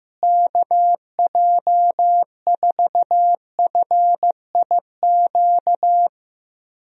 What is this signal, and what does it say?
Morse code, heard as a keyed tone, about 15 wpm, 700 Hz. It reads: KJ4FIQ
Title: USA Call Signs - 6 Characters - Repeated 3x - 15wpm